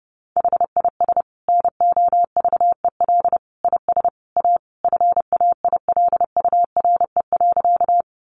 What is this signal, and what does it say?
Morse code, heard as a keyed tone, about 30 wpm, 700 Hz. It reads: HIS NOVEL IS A FAILURE.
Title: Sentences from Top 1300 Words 30wpm